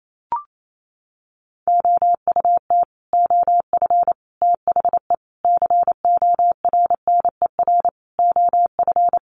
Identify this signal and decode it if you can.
Morse code, heard as a keyed tone, about 28 wpm, 700 Hz. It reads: OUT OF THE CORNER OF
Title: Sets of 5 Words 28wpm